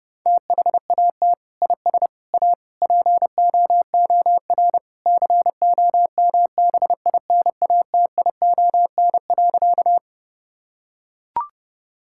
Morse code, heard as a keyed tone, about 30 wpm, 700 Hz. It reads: THAT IS A POOR COMBINATION.
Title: Sentences from Top 1000 Words - Review Part 6 - 30wpm